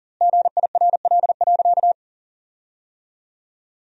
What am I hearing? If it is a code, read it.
Morse code, heard as a keyed tone, about 37 wpm, 700 Hz. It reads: GIRL.
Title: 2x Word-Spacing - Sentences from Top 400 Words - 40wpm